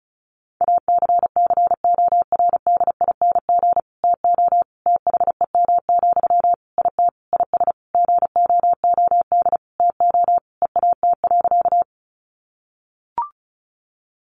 Morse code, heard as a keyed tone, about 34 wpm, 700 Hz. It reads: ACCORDING TO THEM, IT IS GOOD TO EAT.
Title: Sentences from Top 800 Words 35wpm